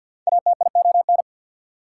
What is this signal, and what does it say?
Morse code, heard as a keyed tone, about 67 wpm, spaced as for 58 wpm, 690 Hz. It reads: ATION
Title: Speed-Racing Prefixes and Suffixes 50wpm